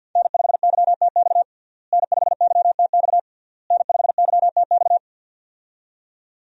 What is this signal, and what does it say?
Morse code, heard as a keyed tone, about 48 wpm, 700 Hz. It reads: N5YTX N5YTX N5YTX
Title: USA Call Signs - 5 Characters - Repeated 3x - 50wpm